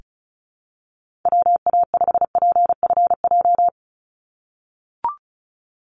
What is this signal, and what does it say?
Morse code, heard as a keyed tone, about 34 wpm, 700 Hz. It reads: WA5PFJ
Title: Call Signs - 35wpm